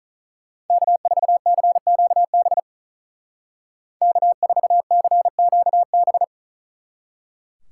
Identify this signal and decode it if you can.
Morse code, heard as a keyed tone, about 47 wpm, 700 Hz. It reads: K4CQB K4CQB